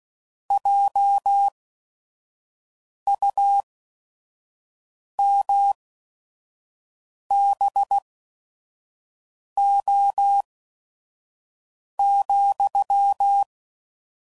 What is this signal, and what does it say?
Morse code, heard as a keyed tone, about 16 wpm, spaced as for 5 wpm, 800 Hz. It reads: JUMBO,